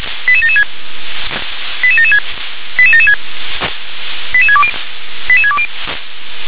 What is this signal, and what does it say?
Five-tone selective call (ZVEI), heard as quick runs of five tones, tones of about 70 ms. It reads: 88776 88776 88776 88620 88620